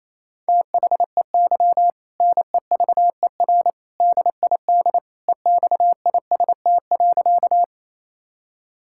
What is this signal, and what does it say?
Morse code, heard as a keyed tone, about 28 wpm, 700 Hz. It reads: THEY NEVER DID EXIST.
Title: Sentences from Top 1000 Words - Review Part 5 - 28wpm